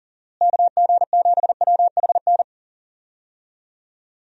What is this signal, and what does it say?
Morse code, heard as a keyed tone, about 40 wpm, 700 Hz. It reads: KG7WHN
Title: USA Call Signs - 6 Characters - Repeated 2x - 40wpm